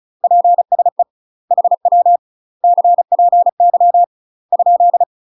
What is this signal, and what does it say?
Morse code, heard as a keyed tone, about 35 wpm, 700 Hz. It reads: PSE HW CPY ?
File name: UK MASTER QSOs TRAINING / 35WPM / UK MASTER QSOs 0022